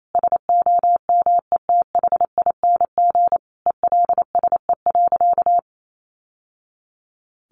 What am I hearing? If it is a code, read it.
Morse code, heard as a keyed tone, about 28 wpm, 700 Hz. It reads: SOMETHING ELSE.
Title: Sentences from Top 800 Words 28wpm